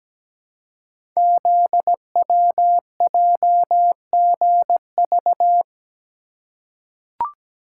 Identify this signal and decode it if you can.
Morse code, heard as a keyed tone, about 17 wpm, 700 Hz. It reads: ZWJGV
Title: Code Groups - 5 Letters - 17wpm